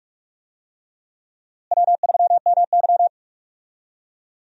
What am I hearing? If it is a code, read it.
Morse code, heard as a keyed tone, about 42 wpm, 700 Hz. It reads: W3MY
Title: USA Call Signs - 4 Characters - Repeated 3x - 45wpm